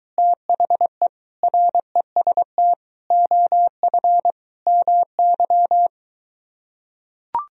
Morse code, heard as a keyed tone, about 23 wpm, 700 Hz. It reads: THE REST OF MY